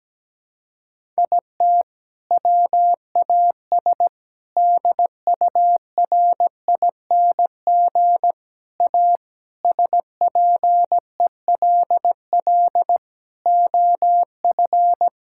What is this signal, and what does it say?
Morse code, heard as a keyed tone, about 17 wpm, 700 Hz. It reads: IT WAS DURING A SPELL OF
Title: Sentences from Top 1000 Words - Review Part 5 - 17wpm